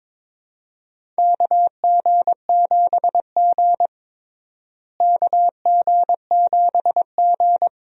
Morse code, heard as a keyed tone, about 21 wpm, 700 Hz. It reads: KG7G KG7G
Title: USA Call Signs - 4 Characters - Repeated 2x - 22wpm